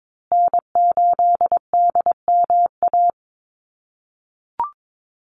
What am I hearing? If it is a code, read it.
Morse code, heard as a keyed tone, about 22 wpm, 700 Hz. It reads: N8DMA
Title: USA Call Signs Repeated 3x 22wpm